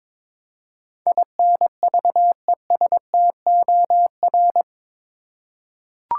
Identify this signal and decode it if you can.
Morse code, heard as a keyed tone, about 22 wpm, 700 Hz. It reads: INVESTOR
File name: practice / 22wpm / Eight Letter Words - 22wpm